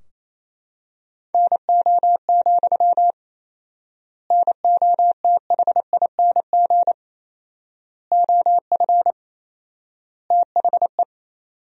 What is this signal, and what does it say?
Morse code, heard as a keyed tone, about 28 wpm, 700 Hz. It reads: NO, NOTHING OF THE
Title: 4x Word-Spacing - Sentences from Top 500 Words - 28wpm